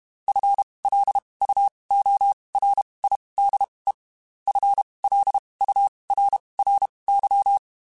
Morse code, heard as a keyed tone, about 32 wpm, spaced as for 23 wpm, 800 Hz. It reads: FLUORIDE FLURRY
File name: Spaced / FL Runs sp mp3s / FL Runs sp @30